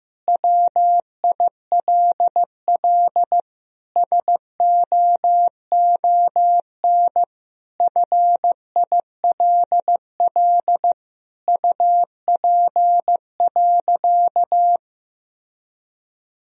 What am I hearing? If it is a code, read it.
Morse code, heard as a keyed tone, about 16 wpm, 700 Hz. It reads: WILL SOON FILL UP.
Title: Sentences from Top 1000 Words - Review Part 3 - 15wpm